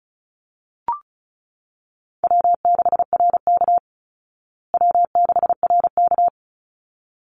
Morse code, heard as a keyed tone, about 34 wpm, 700 Hz. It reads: W6RK W6RK
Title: Call Signs Repeated 3x - 35wpm